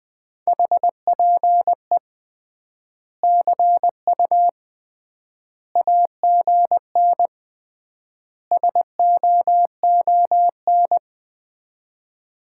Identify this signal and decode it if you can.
Morse code, heard as a keyed tone, about 20 wpm, 700 Hz. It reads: HPE CU AGN SOON
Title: Common QSO Phrases - 3x Word-Spacing - 20wpm